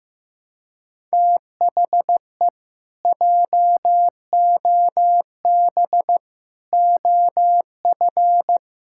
Morse code, heard as a keyed tone, about 15 wpm, 700 Hz. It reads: THE JOB OF
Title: Pangram Challenge v2 - 15wpm